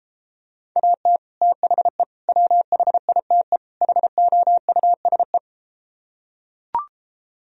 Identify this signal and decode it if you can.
Morse code, heard as a keyed tone, about 33 wpm, 700 Hz. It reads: AT THE WHITE HOUSE